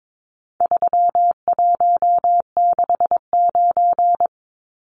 Morse code, heard as a keyed tone, about 22 wpm, 700 Hz. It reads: 3169
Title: Numbers - 4-Digit 22wpm